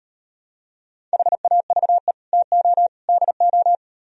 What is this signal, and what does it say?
Morse code, heard as a keyed tone, about 38 wpm, 700 Hz. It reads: HAVE TO DO